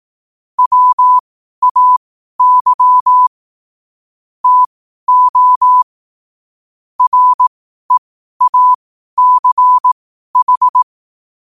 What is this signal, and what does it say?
Morse code, heard as a keyed tone, about 18 wpm, spaced as for 12 wpm, 1000 Hz. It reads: WAY TO REACH
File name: qotdfiles / 12wpm